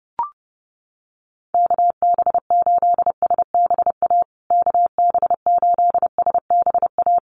Morse code, heard as a keyed tone, about 30 wpm, 700 Hz. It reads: KB8SBA KB8SBA
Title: USA Call Signs Repeated 3x 30wpm